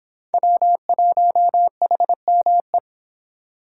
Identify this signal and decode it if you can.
Morse code, heard as a keyed tone, about 31 wpm, 700 Hz. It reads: W1HME